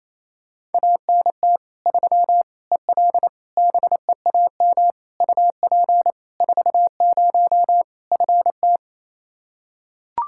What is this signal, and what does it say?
Morse code, heard as a keyed tone, about 27 wpm, 700 Hz. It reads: ANT 3 EL BEAM UP 40 FT
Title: Practice QSO - Exchange 2 - Info - 28wpm